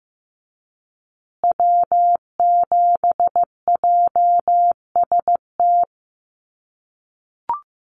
Morse code, heard as a keyed tone, about 15 wpm, 700 Hz. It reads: W7JST
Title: USA Call Signs Repeated 2x 15wpm